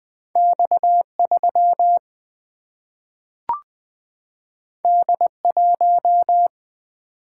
Morse code, heard as a keyed tone, about 20 wpm, 700 Hz. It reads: X 3 D 1
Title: Single-Letter Single-Number 20wpm